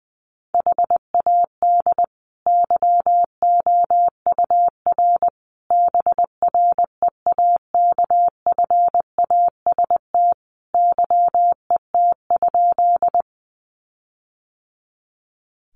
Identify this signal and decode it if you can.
Morse code, heard as a keyed tone, about 20 wpm, 700 Hz. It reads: HAD YOUR BREAKFAST YET?
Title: Sentences from Top 1500 Words 20wpm